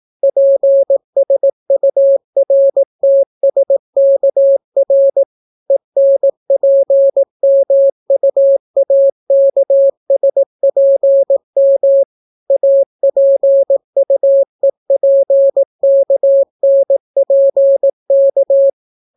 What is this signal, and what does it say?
Morse code, heard as a keyed tone, about 18 wpm, 550 Hz. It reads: PSURTSKR ENPMUAKSPM APUEPKNPK